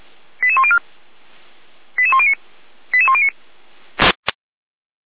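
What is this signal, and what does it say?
Five-tone selective call (ZVEI), heard as quick runs of five tones, tones of about 70 ms. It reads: 88104 88109 88109
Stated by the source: ZVEI five-tone call received by SDR, archived by fire brigade Feuerwehr Orlamünde.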